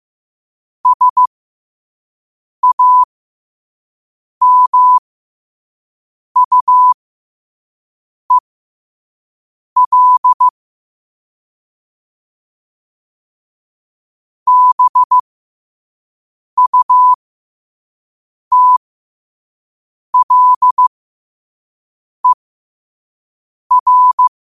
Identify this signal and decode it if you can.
Morse code, heard as a keyed tone, about 15 wpm, spaced as for 5 wpm, 1000 Hz. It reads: SAMUEL BUTLER
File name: qotdfiles / 5wpm